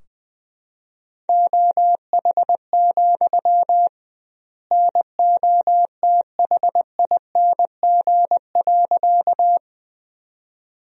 Morse code, heard as a keyed tone, about 20 wpm, 700 Hz. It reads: OH, NOTHING.
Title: 2x Word-Spacing - Sentences from Top 500 Words - 20wpm